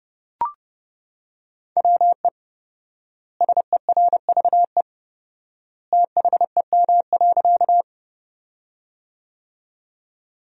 Morse code, heard as a keyed tone, about 32 wpm, 700 Hz. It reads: WE SERVE THEM.
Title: 4x Word-Spacing - Sentences from Top 400 Words - 30wpm